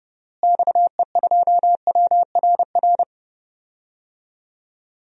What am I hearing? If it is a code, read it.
Morse code, heard as a keyed tone, about 30 wpm, 700 Hz. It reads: XE2WRR
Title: DXCC-100 Call Signs - 6 Characters - Repeated 2x - 30wpm